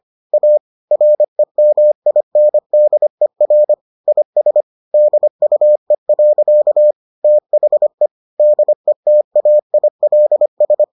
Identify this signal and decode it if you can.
Morse code, heard as a keyed tone, about 25 wpm, 600 Hz. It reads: A REMINDER IS DUE. THE DETAILS